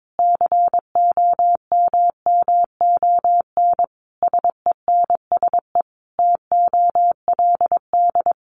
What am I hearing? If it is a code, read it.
Morse code, heard as a keyed tone, about 22 wpm, 700 Hz. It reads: COMMON SENSE TOLD
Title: Sentences from Top 1200 Words 22wpm